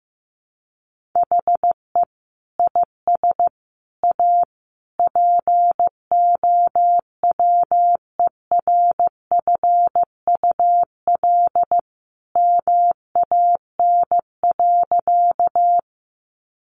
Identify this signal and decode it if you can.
Morse code, heard as a keyed tone, about 14 wpm, 700 Hz. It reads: HE IS A POWERFUL MAN.
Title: Sentences from Top 1300 Words 15wpm